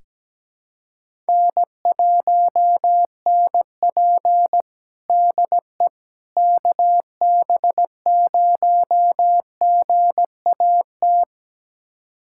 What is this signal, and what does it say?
Morse code, heard as a keyed tone, about 17 wpm, 700 Hz. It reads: N1NP DE KB0GAT
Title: USA Callsign DE Callsign 17wpm